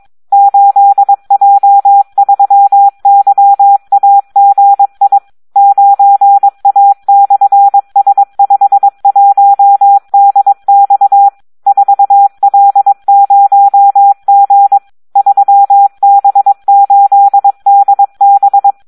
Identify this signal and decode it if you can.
Morse code, heard as a keyed tone, about 22 wpm, 800 Hz, with radio noise. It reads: 8J3YAGI 9A/S51DX 4L0G 3B8DB
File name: Audio practica CW / SD 1 22